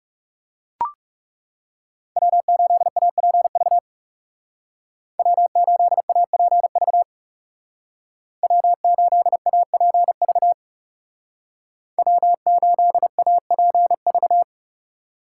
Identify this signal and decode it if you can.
Morse code, heard as a keyed tone, about 37 wpm, 700 Hz. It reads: W8APV W8APV W8APV W8APV
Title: Speed Racing - USA 1x3 Call Signs 30wpm